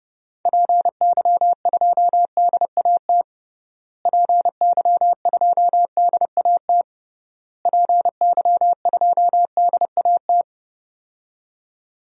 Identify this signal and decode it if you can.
Morse code, heard as a keyed tone, about 30 wpm, 700 Hz. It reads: PY2DAT PY2DAT PY2DAT
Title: DXCC-100 Call Signs - 6 Characters - Repeated 3x - 30wpm